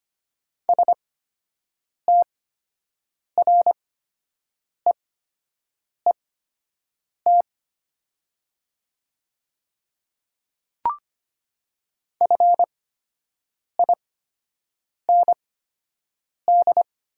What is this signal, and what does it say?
Morse code, heard as a keyed tone, about 25 wpm, spaced as for 7 wpm, 700 Hz. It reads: STREET FIND
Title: 8x Character Spacing - Top 500 Words - 25wpm